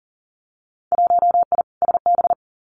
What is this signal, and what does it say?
Morse code, heard as a keyed tone, about 40 wpm, 700 Hz. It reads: 1I SB
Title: Winter Field Day Exchanges Encore 40wpm